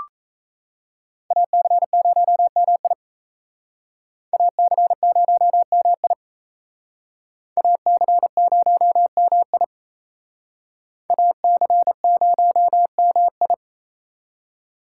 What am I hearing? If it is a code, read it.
Morse code, heard as a keyed tone, about 35 wpm, 700 Hz. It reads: AC0MI AC0MI AC0MI AC0MI
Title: Speed Racing - USA 2x2 Call Signs 28wpm